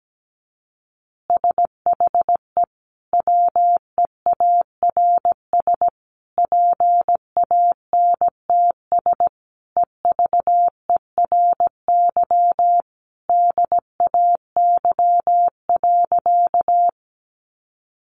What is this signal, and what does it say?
Morse code, heard as a keyed tone, about 17 wpm, 700 Hz. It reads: SHE WEARS PANTS EVERY DAY.
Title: Additional Easy Phrases 17wpm